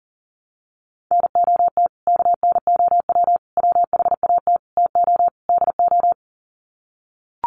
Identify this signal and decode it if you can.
Morse code, heard as a keyed tone, about 37 wpm, 700 Hz. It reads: NOT KNOW WHAT TO DO
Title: Sets of 5 Words 40wpm